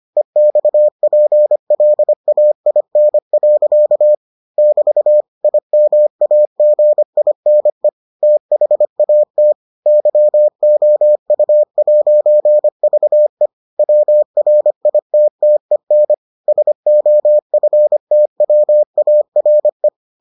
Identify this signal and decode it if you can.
Morse code, heard as a keyed tone, about 25 wpm, 600 Hz. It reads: EXPLAIN. <BT> IMAGINE THAT YOU'VE WRITTEN SOFTWARE